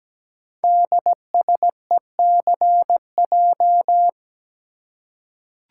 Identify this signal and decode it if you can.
Morse code, heard as a keyed tone, about 17 wpm, 700 Hz. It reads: DSECJ
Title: Code Groups - 5 Letters - 17wpm